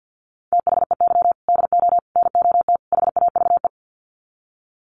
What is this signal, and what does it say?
Morse code, heard as a keyed tone, about 49 wpm, 700 Hz. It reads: THEY DO NOT HAVE
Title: Sets of 4 Words 50wpm